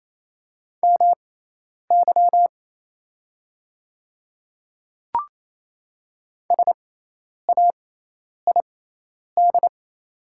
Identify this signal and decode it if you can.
Morse code, heard as a keyed tone, about 27 wpm, spaced as for 10 wpm, 690 Hz. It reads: MY SAID
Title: 6x Character Spacing - Top 100 Words - 28wpm